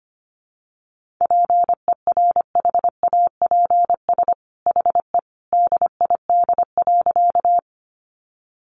Morse code, heard as a keyed tone, about 25 wpm, 700 Hz. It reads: PERHAPS HE DID.
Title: Sentences from Top 500 Words 25wpm